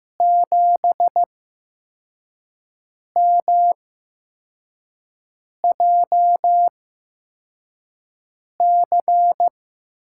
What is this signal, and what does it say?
Morse code, heard as a keyed tone, about 15 wpm, spaced as for 4 wpm, 700 Hz. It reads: 7MJC
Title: USA Call Signs - 8x Character Spacing - 15wpm